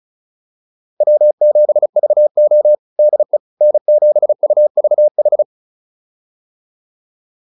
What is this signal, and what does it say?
Morse code, heard as a keyed tone, about 34 wpm, 600 Hz. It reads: W7VO DE N7UVH
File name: QSO105 35